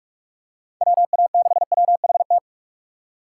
Morse code, heard as a keyed tone, about 46 wpm, 700 Hz. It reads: WA6WHT
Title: USA Call Signs - 2x Character Spacing - 45wpm